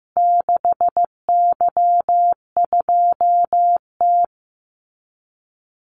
Morse code, heard as a keyed tone, about 15 wpm, 700 Hz. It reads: 6Y2T
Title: Call Signs Repeated 3x - 15wpm